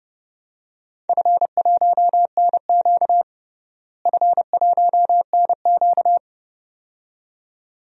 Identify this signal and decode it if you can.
Morse code, heard as a keyed tone, about 30 wpm, 700 Hz. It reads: F1NQ F1NQ
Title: DXCC-100 Call Signs - 4 Characters - Repeated 2x - 30wpm